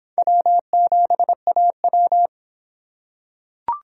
Morse code, heard as a keyed tone, about 26 wpm, 700 Hz. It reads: W7AW